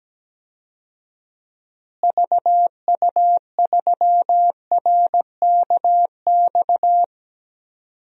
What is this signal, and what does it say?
Morse code, heard as a keyed tone, about 17 wpm, 700 Hz. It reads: VU3RKX
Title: DXCC-100 Call Signs - 6 Characters - Repeated 3x - 17wpm